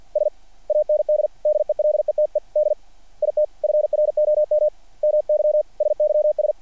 Morse code, heard as a keyed tone, about 48 wpm, 600 Hz. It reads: S AND DELETED IT FROM MY SYS
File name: archived / CRSnet